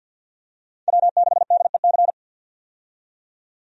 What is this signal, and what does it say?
Morse code, heard as a keyed tone, about 52 wpm, 700 Hz. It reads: W6DEC
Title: USA Call Signs - 5 Characters - 50wpm